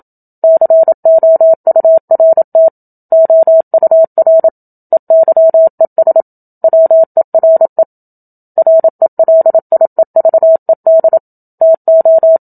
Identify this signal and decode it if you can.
Morse code, heard as a keyed tone, about 27 wpm, 650 Hz. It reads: COURT OUR EYES WERE RELIEVED TO
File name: Book Two / 01-Chapter 1 - 27 WPM